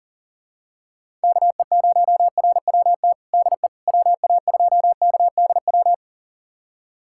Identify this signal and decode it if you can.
Morse code, heard as a keyed tone, about 40 wpm, 700 Hz. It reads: KE0RWT DE WA2KDW
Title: USA Callsign DE Callsign - 6 Characters - 40wpm